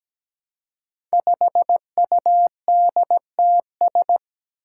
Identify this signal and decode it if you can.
Morse code, heard as a keyed tone, about 17 wpm, 700 Hz. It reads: 5UDTS
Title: Five Challenging Letter-Number Combinations 17wpm